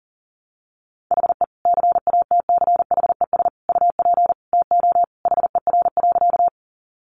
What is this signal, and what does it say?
Morse code, heard as a keyed tone, about 40 wpm, 700 Hz. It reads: HE CATCHES UP TO HER.
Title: Additional Easy Phrases 40wpm